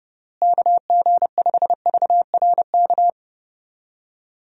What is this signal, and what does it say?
Morse code, heard as a keyed tone, about 30 wpm, 700 Hz. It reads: KG5VRK
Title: USA Call Signs - 6 Characters - Repeated 2x - 30wpm